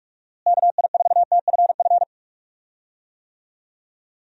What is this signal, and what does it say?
Morse code, heard as a keyed tone, about 46 wpm, 700 Hz. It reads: KI4TFF
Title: USA Call Signs - 6 Characters - Repeated 3x - 45wpm